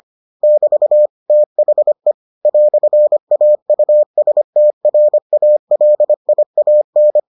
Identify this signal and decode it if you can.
Morse code, heard as a keyed tone, about 25 wpm, 600 Hz. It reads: <BT> THE "AUSTRALIAN